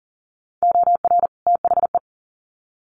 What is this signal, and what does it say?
Morse code, heard as a keyed tone, about 40 wpm, 700 Hz. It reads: OR THE
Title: Sets of 2 Words 40wpm